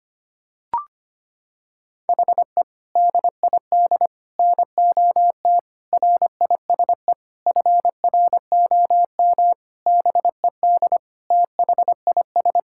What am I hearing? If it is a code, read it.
Morse code, heard as a keyed tone, about 25 wpm, 700 Hz. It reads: HE DID NOT RISE FROM BED THIS
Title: Sentences from Top 1000 Words - Review Part 6 - 25wpm